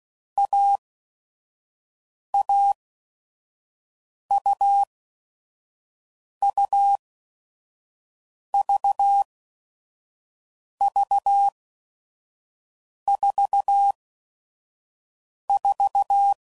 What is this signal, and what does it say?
Morse code, heard as a keyed tone, about 16 wpm, spaced as for 5 wpm, 800 Hz. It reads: AAUUVV44
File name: Combinations / Difficult Combos / AUV AUV4